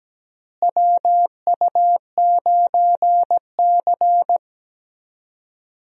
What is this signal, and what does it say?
Morse code, heard as a keyed tone, about 17 wpm, 700 Hz. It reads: WU9C
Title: USA Call Signs - 4 Characters - Repeated 2x - 17wpm